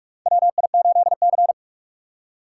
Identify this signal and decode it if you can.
Morse code, heard as a keyed tone, about 41 wpm, 700 Hz. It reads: WI8C
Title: USA Call Signs - 4 Characters - 6x Character Spacing - 45wpm